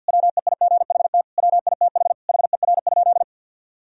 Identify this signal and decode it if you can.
Morse code, heard as a keyed tone, about 50 wpm, 700 Hz. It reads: WEIGHT WITH HER?
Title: Sentences from Top 1000 Words - Review Part 3 - 50wpm